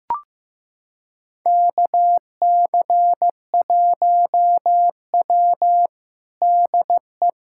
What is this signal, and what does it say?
Morse code, heard as a keyed tone, about 15 wpm, 700 Hz. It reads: KC1W DE
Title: USA Callsign DE Callsign - 4 Characters - 15wpm